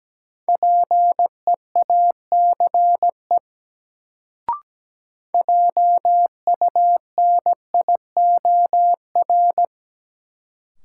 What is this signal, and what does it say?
Morse code, heard as a keyed tone, about 17 wpm, 700 Hz. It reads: PEACE JUNIOR
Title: Sentences from Top 1500 Words - New Word Drill Alt - 17wpm